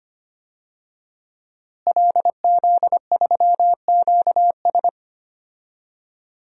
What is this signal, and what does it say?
Morse code, heard as a keyed tone, about 25 wpm, 700 Hz. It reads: LZ3QS
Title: DXCC-100 Call Signs - 5 Characters - Repeated 3x - 25wpm